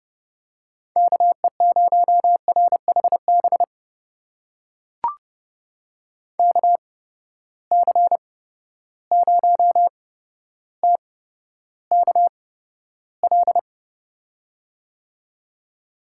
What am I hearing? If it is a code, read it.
Morse code, heard as a keyed tone, about 29 wpm, spaced as for 13 wpm, 700 Hz. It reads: KE0RHB KC0TKL
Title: USA Call Signs - 6 Characters - 8x Character Spacing - 30wpm